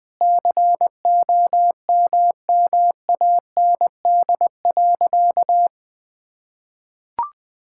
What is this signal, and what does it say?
Morse code, heard as a keyed tone, about 20 wpm, 700 Hz. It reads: COMMAND.
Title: Sentences from Top 1600 Words 20wpm